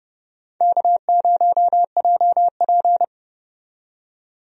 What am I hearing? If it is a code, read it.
Morse code, heard as a keyed tone, about 30 wpm, 700 Hz. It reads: K0JP
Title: USA Call Signs - 4 Characters - 30wpm